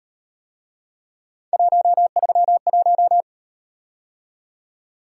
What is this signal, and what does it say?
Morse code, heard as a keyed tone, about 38 wpm, 700 Hz. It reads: 131